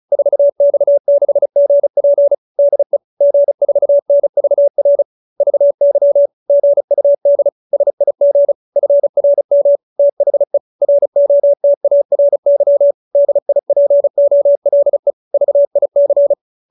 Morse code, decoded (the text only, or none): 4X6GP DE G4NVR VY GUD SIG FRM THE ROTARY DIPOLE VIC